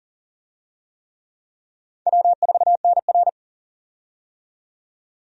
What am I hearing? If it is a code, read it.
Morse code, heard as a keyed tone, about 37 wpm, 700 Hz. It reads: W4NR
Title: USA Call Signs - 4 Characters - 40wpm